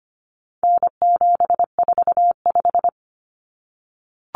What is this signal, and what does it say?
Morse code, heard as a keyed tone, about 25 wpm, 700 Hz. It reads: N745
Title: Common Cut Numbers - 4-Digit 25wpm